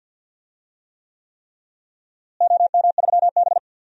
Morse code, heard as a keyed tone, about 50 wpm, 700 Hz. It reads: OM3B
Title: DXCC-100 Call Signs - 4 Characters - Repeated 3x - 50wpm